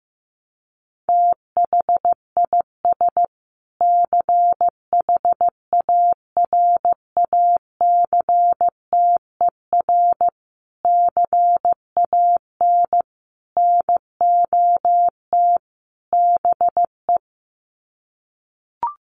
Morse code, heard as a keyed tone, about 15 wpm, 700 Hz. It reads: THIS CHARACTER CAN NOT BE
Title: Sets of 5 Words Encore2 15wpm